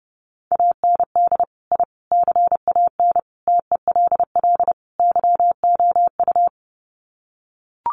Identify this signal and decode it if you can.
Morse code, heard as a keyed tone, about 30 wpm, 700 Hz. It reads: AND I CAN TELL YOU
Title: Sets of 5 Words Encore2 30wpm